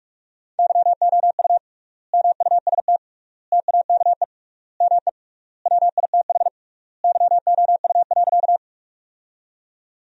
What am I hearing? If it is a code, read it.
Morse code, heard as a keyed tone, about 46 wpm, 700 Hz. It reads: YOU MUST TAKE ME WITH YOU.
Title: 3x Word-Spacing - Sentences from Top 200 Words - 45wpm